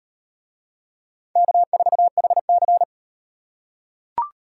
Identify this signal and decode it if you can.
Morse code, heard as a keyed tone, about 38 wpm, 700 Hz. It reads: K4HC